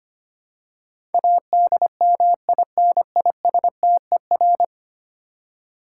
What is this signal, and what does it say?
Morse code, heard as a keyed tone, about 25 wpm, 700 Hz. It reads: ADMINISTER
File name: Ten Letter Words - 25wpm